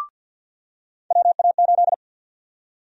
Morse code, heard as a keyed tone, about 49 wpm, 700 Hz. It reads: WA8
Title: USA Call Sign Prefixes - 50wpm